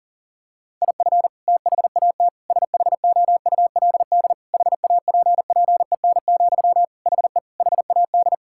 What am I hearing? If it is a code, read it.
Morse code, heard as a keyed tone, about 40 wpm, 700 Hz. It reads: IF THAT SHOULD HAPPEN, HE HAD
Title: Sentences from Top 1000 Words - Review Part 6 - 40wpm